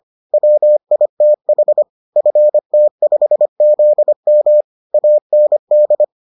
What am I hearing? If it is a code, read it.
Morse code, heard as a keyed tone, about 25 wpm, 600 Hz. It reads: WITH FT5ZM AND